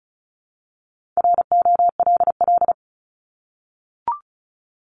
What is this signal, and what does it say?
Morse code, heard as a keyed tone, about 34 wpm, 700 Hz. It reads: ROLL
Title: Top 1751-2000 Words 35wpm